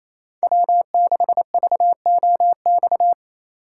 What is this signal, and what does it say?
Morse code, heard as a keyed tone, about 36 wpm, 700 Hz. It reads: W6VOX